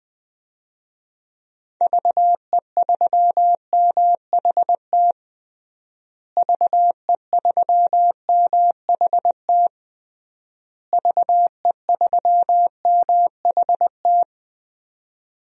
Morse code, heard as a keyed tone, about 20 wpm, 700 Hz. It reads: VE3MHT VE3MHT VE3MHT
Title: DXCC-100 Call Signs - 6 Characters - Repeated 3x - 20wpm